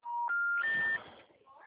Special information tone, with noise.